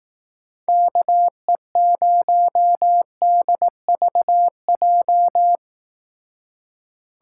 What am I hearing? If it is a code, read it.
Morse code, heard as a keyed tone, about 18 wpm, 700 Hz. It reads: KE0DVJ
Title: Speed Racing - USA 2x3 Call Signs 15wpm